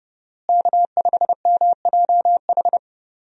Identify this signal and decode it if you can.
Morse code, heard as a keyed tone, about 31 wpm, 700 Hz. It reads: K5MJH
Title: USA Call Signs - 5 Characters - Repeated 2x - 30wpm